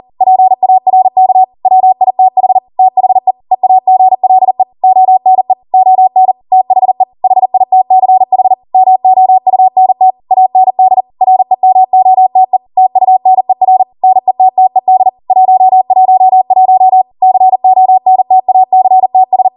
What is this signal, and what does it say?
Morse code, heard as a keyed tone, about 40 wpm, 750 Hz. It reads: PARK WITH THE EAGLE ONE ON THE HITCH MOUNT AND REMOTE TUNER NETTED 111 CONTACTS